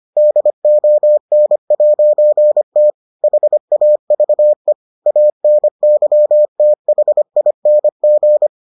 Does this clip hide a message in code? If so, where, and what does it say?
Yes: Morse code from 0.2 to 8.5 s, heard as a keyed tone, about 25 wpm, 600 Hz. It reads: DON'T HAVE ANYTHING